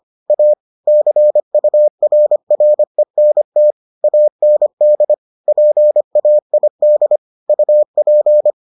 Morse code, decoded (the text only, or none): A CURRENT AND PAID UP